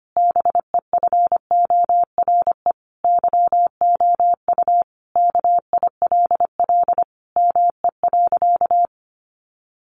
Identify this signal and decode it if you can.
Morse code, heard as a keyed tone, about 25 wpm, 700 Hz. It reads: BEFORE YOU KILL ME.
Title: Sentences from Top 1400 Words 25wpm